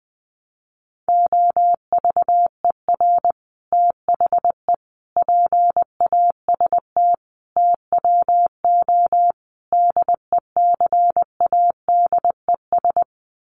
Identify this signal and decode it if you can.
Morse code, heard as a keyed tone, about 20 wpm, 700 Hz. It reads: OVER THE PAST TWO DECADES
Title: Sets of 5 Words Encore2 20wpm